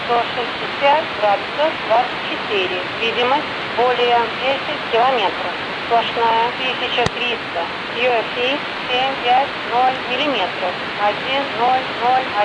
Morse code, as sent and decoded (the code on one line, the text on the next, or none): none